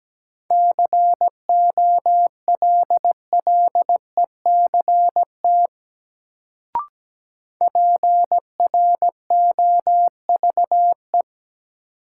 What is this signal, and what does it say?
Morse code, heard as a keyed tone, about 17 wpm, 700 Hz. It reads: COLLECT PROVE
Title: Sentences from Top 1600 Words - New Word Drill Alt - 17wpm